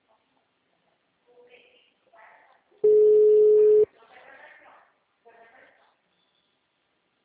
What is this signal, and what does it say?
Ringback tone (Europe, 425 Hz), with speech.
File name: out- -2010- - - .421